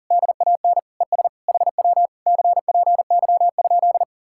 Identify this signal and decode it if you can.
Morse code, heard as a keyed tone, about 50 wpm, 700 Hz. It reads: DAN ES HW CPY?
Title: Speed-Racing Common QSO Phrases - 40wpm